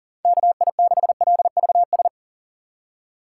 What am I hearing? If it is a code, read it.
Morse code, heard as a keyed tone, about 40 wpm, 700 Hz. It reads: KI6LVS
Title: USA Call Signs - 6 Characters - 2x Character Spacing - 40wpm